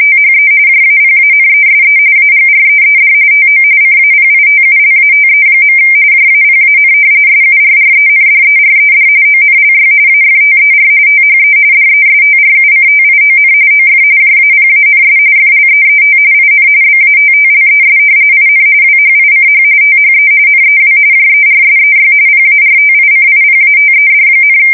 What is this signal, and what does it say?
RTTY, about 45.45 baud. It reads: THE JAPANESE AGREED TO CAPITULATE. JAPAN SURRENDERED ONLY AFTER THE DESTRUCTION OF JAPANS FLEET IN BATTLES IN WHICH THE NEW JERSEY PARTICIPATED,
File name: fsk170 / battleshipnj